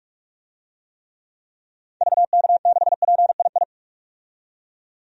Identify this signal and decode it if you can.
Morse code, heard as a keyed tone, about 41 wpm, 700 Hz. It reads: VK6PII
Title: DXCC-100 Call Signs - 6 Characters - 45wpm